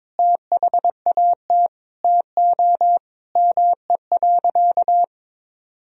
Morse code, heard as a keyed tone, about 22 wpm, 700 Hz. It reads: THAT TO ME.
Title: Sentences from Top 400 Words and Abbreviations 22wpm